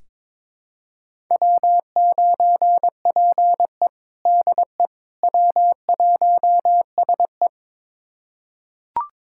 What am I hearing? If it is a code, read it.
Morse code, heard as a keyed tone, about 22 wpm, 700 Hz. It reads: W9PE DE W1SE
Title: USA Callsign DE Callsign - 4 Characters - 22wpm